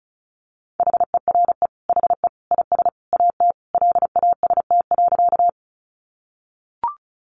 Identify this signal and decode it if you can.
Morse code, heard as a keyed tone, about 34 wpm, 700 Hz. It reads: HERE HE IS AT LAST.
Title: Sentences from Top 300 Words 35wpm